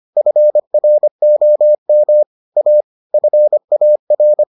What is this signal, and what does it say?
Morse code, heard as a keyed tone, about 25 wpm, 600 Hz. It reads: FROM A FAR